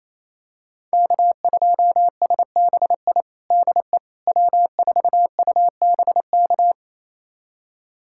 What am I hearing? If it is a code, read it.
Morse code, heard as a keyed tone, about 27 wpm, 700 Hz. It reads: K2SBI DE W4UBK
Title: USA Callsign DE Callsign - 5 Characters - 28wpm